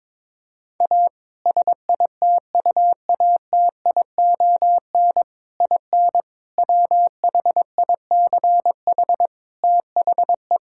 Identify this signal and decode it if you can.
Morse code, heard as a keyed tone, about 21 wpm, 700 Hz. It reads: A SITUATION IN WHICH THE
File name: practice / 22wpm / Sentences with 3-Word Phrases - 22wpm